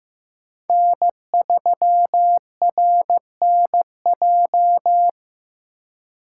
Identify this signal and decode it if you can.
Morse code, heard as a keyed tone, about 15 wpm, 700 Hz. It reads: N3RNJ
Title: USA Call Signs - 5 Characters - 3x Character Spacing - 15wpm